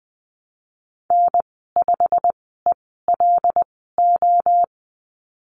Morse code, heard as a keyed tone, about 20 wpm, spaced as for 14 wpm, 700 Hz. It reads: N5ELO
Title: 122 Lesson 20g - Number 3 - Call Signs 20wpm